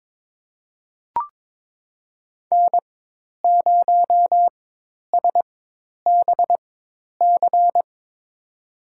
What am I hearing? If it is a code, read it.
Morse code, heard as a keyed tone, about 21 wpm, spaced as for 10 wpm, 700 Hz. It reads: N0SBC
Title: USA Call Signs - 5 Characters - 4x Character Spacing - 22wpm